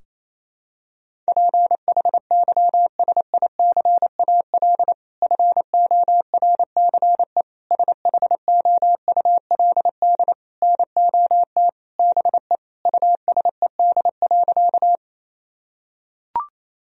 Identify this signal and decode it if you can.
Morse code, heard as a keyed tone, about 28 wpm, 700 Hz. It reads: PHYSICAL FORCE SHOULD NOT BE USED.
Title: Sentences from Top 1000 Words - Review Part 4 - 28wpm